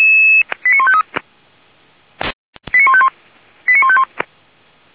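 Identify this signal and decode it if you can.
Five-tone selective call (ZVEI), heard as quick runs of five tones, tones of about 80 ms. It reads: 89153 89151 89151
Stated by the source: ZVEI five-tone selective call 85860 received on an SDR.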